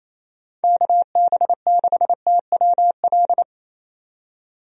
Morse code, heard as a keyed tone, about 28 wpm, 700 Hz. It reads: KB6TWL
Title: USA Call Signs - 6 Characters - 28wpm